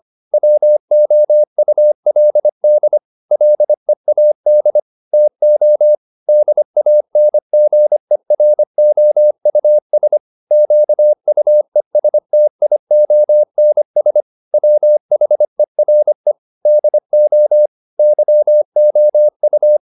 Morse code, heard as a keyed tone, about 25 wpm, 600 Hz. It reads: WOULD LEAD TO DANGEROUS QUESTIONS WHERE DO YOU